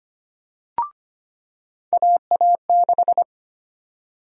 Morse code, heard as a keyed tone, about 25 wpm, 700 Hz. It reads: AA6
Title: USA Call Sign Prefixes - 25wpm